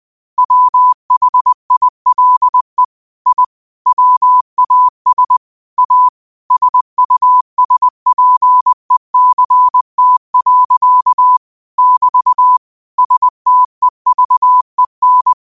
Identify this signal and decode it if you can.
Morse code, heard as a keyed tone, about 20 wpm, 1000 Hz. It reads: WHILE I WAS A SUSPECT. <BT> STEVEN
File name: qotdfiles / 20wpm